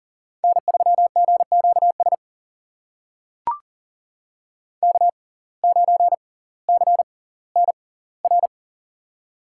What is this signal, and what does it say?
Morse code, heard as a keyed tone, about 40 wpm, spaced as for 20 wpm, 700 Hz. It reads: N3GQS K9CNR
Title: USA Call Signs - 5 Characters - 6x Character Spacing - 40wpm